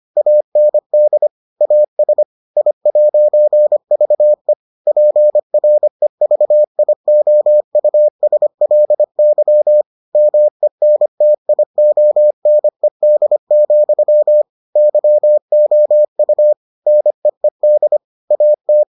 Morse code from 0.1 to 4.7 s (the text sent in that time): AND AS I'VE